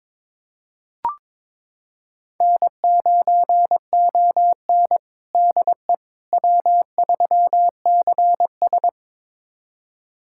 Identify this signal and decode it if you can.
Morse code, heard as a keyed tone, about 22 wpm, 700 Hz. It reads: N9ON DE W3CS
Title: USA Callsign DE Callsign - 4 Characters - 22wpm